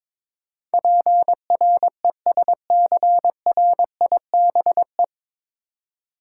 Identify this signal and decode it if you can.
Morse code, heard as a keyed tone, about 22 wpm, 700 Hz. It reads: PRESCRIBE
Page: Morse Code Ninja practice 22wpm